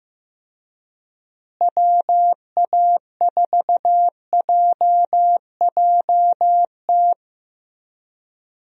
Morse code, heard as a keyed tone, about 16 wpm, 700 Hz. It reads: WA4JJT
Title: USA Call Signs - 6 Characters - Repeated 2x - 15wpm